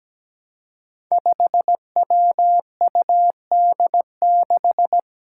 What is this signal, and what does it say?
Morse code, heard as a keyed tone, about 17 wpm, 700 Hz. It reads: 5WUD6
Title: Five Challenging Letter-Number Combinations 17wpm